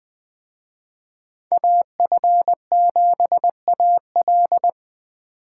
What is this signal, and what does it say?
Morse code, heard as a keyed tone, about 24 wpm, 700 Hz. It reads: AF7AL
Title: Speed Racing - USA 2x2 Call Signs 20wpm